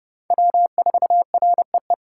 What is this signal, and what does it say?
Morse code, heard as a keyed tone, about 30 wpm, 700 Hz. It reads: W4REE
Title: USA Callsign DE Callsign - 5 Characters - 30wpm